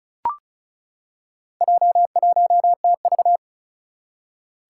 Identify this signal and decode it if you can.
Morse code, heard as a keyed tone, about 36 wpm, 700 Hz. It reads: J1TV
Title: Four Challenging Letter-Number Combinations 35wpm